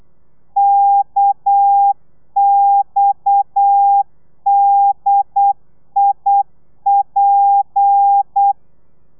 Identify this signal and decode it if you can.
Morse code, heard as a keyed tone, about 8 wpm, 800 Hz. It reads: KXDIP